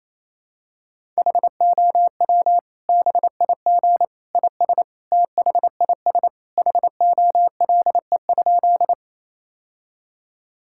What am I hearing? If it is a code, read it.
Morse code, heard as a keyed tone, about 28 wpm, 700 Hz. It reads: HOW BIG IS THIS HOLE?
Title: Sentences from Top 1600 Words 28wpm